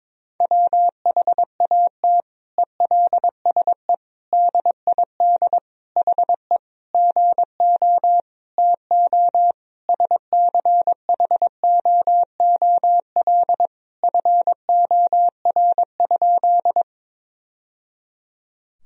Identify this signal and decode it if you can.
Morse code, heard as a keyed tone, about 21 wpm, 700 Hz. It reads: WHAT ELSE DID HE GO TO SCHOOL FOR?
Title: Sentences from Top 1000 Words - Review Part 1 - 22wpm